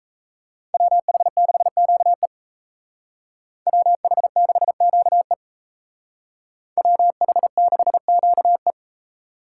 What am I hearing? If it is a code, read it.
Morse code, heard as a keyed tone, about 32 wpm, 700 Hz. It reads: WH6QE WH6QE WH6QE
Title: Speed Racing - USA 2x2 Call Signs 28wpm